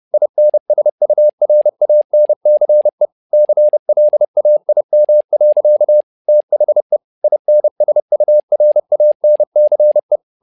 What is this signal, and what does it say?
Morse code, heard as a keyed tone, about 30 wpm, 600 Hz. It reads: INSURANCE CLAIM. THE INSURANCE
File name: learning center / tower trouble / tower trouble 30wpm